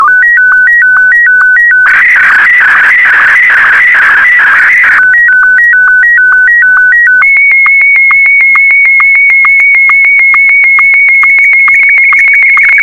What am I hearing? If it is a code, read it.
SSTV (slow-scan television), Martin 1.